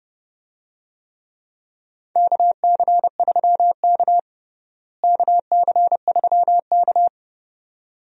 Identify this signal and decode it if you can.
Morse code, heard as a keyed tone, about 30 wpm, 700 Hz. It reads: KC3K KC3K
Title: USA Call Signs - 4 Characters - Repeated 2x - 30wpm